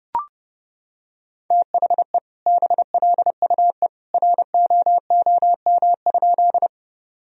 Morse code, heard as a keyed tone, about 30 wpm, 700 Hz. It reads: THE BLUE ROOM?
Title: Sentences from Top 1000 Words - Review Part 3 - 30wpm